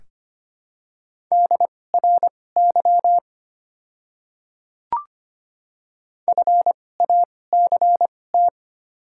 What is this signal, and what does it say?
Morse code, heard as a keyed tone, about 25 wpm, spaced as for 18 wpm, 700 Hz. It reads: DRY FACT
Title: 2x Character Spacing - Top 500 Words - 25wpm